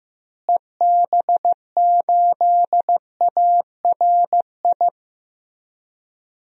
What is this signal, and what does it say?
Morse code, heard as a keyed tone, about 15 wpm, 700 Hz. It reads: EB8ARI
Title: DXCC-100 Call Signs - 6 Characters - Repeated 3x - 15wpm